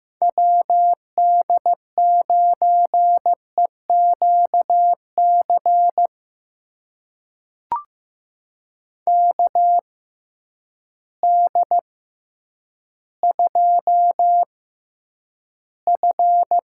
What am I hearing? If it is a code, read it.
Morse code, heard as a keyed tone, about 15 wpm, spaced as for 9 wpm, 700 Hz. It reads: WD9EQC KD2F
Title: USA Call Signs - 6 Characters - 6x Character Spacing - 15wpm